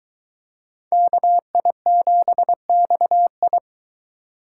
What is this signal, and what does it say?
Morse code, heard as a keyed tone, about 22 wpm, 700 Hz. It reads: KI7XI